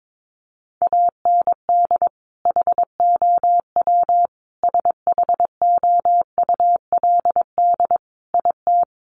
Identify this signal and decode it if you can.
Morse code, heard as a keyed tone, about 22 wpm, 700 Hz. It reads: AND HOW SHOULD IT
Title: Sentences from Top 1200 Words 22wpm